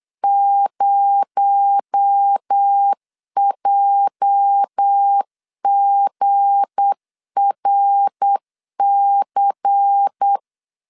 Morse code, heard as a keyed tone, about 8 wpm, 800 Hz. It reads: 0JGRC
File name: LEZIONI / 8 YJ45 / 8 YJ45 8wpm 3214